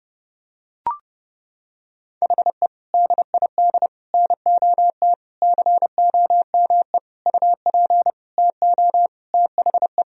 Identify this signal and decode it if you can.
Morse code, heard as a keyed tone, about 30 wpm, 700 Hz. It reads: HE DID NOT COME UP TO THE
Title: Sentences from Top 1000 Words - Review Part 3 - 30wpm